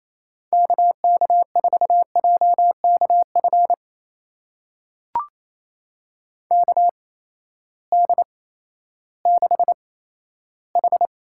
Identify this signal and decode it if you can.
Morse code, heard as a keyed tone, about 27 wpm, spaced as for 14 wpm, 700 Hz. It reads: KK4JKF KD6H
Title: USA Call Signs - 6 Characters - 8x Character Spacing - 28wpm